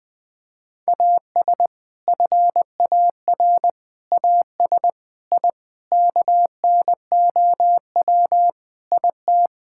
Morse code, heard as a keyed tone, about 19 wpm, 700 Hz. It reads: AS FAR AS I KNOW IT
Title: Sentences with 5-Word Phrases - 20wpm